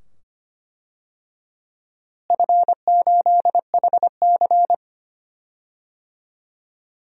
Morse code, heard as a keyed tone, about 25 wpm, 700 Hz. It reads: F8HC
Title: DXCC-100 Call Signs - 4 Characters - Repeated 2x - 25wpm